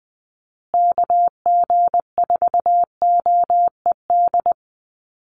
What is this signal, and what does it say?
Morse code, heard as a keyed tone, about 20 wpm, 700 Hz. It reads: KG4OED
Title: USA Call Signs Repeated 3x 20wpm